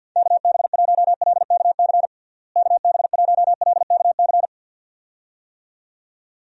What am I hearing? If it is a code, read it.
Morse code, heard as a keyed tone, about 50 wpm, 700 Hz. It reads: KB1LKC KB1LKC
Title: USA Call Signs - 6 Characters - Repeated 2x - 50wpm